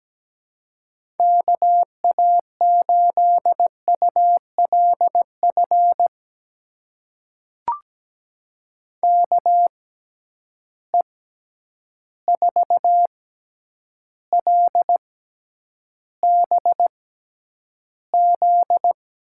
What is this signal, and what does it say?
Morse code, heard as a keyed tone, about 17 wpm, spaced as for 9 wpm, 700 Hz. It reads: KA8ULF KE4LBZ
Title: USA Call Signs - 6 Characters - 6x Character Spacing - 17wpm